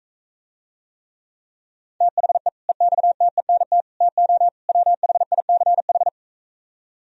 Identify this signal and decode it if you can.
Morse code, heard as a keyed tone, about 45 wpm, 700 Hz. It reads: THE EXTENT TO WHICH